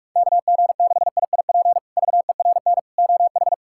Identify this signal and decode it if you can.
Morse code, heard as a keyed tone, about 47 wpm, 700 Hz. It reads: KG6IIP VERN OH
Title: K1USN SST QSO Practice - 45wpm